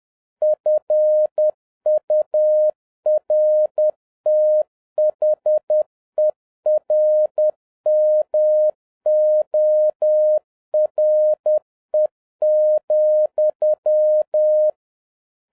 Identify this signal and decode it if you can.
Morse code, heard as a keyed tone, about 10 wpm, 610 Hz. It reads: FURTHERMORE,
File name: 2011-07-mp3 / n41 EXCITING QRP CW OPERATION 10wpm